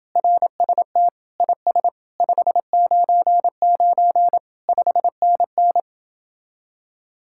Morse code, heard as a keyed tone, about 28 wpm, 700 Hz. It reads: RST IS 599 5NN